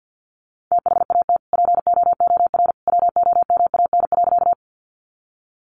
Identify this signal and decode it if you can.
Morse code, heard as a keyed tone, about 50 wpm, 700 Hz. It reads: THAT POOR WOMAN.
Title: Sentences from Top 1100 Words 50wpm